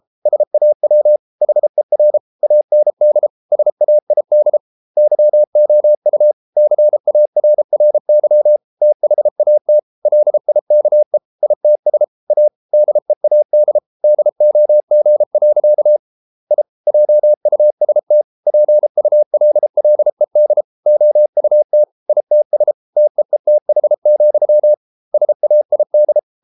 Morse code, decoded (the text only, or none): SAW HER AND SAID YOU CARRY THAT LIKE ITS A DEAD DOG. I JUST PULLED OUT ITS TEETH, SAID